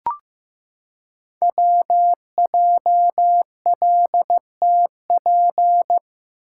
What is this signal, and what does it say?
Morse code, heard as a keyed tone, about 15 wpm, 700 Hz. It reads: WJLTP
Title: Code Groups - 5 Letters - 15wpm